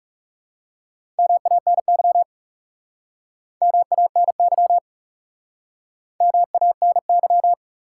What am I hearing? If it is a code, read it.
Morse code, heard as a keyed tone, about 37 wpm, 700 Hz. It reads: MANY MANY MANY
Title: Speed-Racing Top 180 QSO Elements 30wpm